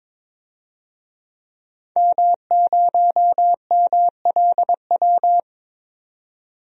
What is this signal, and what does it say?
Morse code, heard as a keyed tone, about 22 wpm, 700 Hz. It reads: M0MLW
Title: DXCC-100 Call Signs - 5 Characters - 22wpm